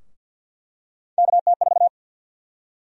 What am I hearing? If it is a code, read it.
Morse code, heard as a keyed tone, about 49 wpm, 700 Hz. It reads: KT4
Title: USA Call Sign Prefixes - Repeated 2x - 50wpm